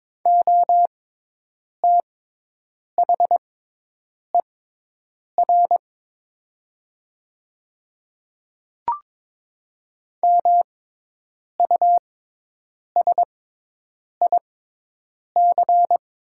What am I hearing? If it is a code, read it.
Morse code, heard as a keyed tone, about 22 wpm, spaced as for 8 wpm, 690 Hz. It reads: OTHER MUSIC
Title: 6x Character Spacing - Top 300 Words - 22wpm